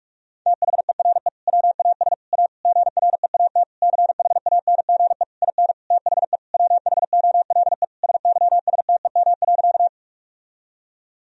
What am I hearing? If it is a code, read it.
Morse code, heard as a keyed tone, about 46 wpm, 700 Hz. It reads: THERE WAS A GREAT CHANGE IN THE WHOLE SYSTEM.
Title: Sentences from Top 1000 Words - Review Part 3 - 45wpm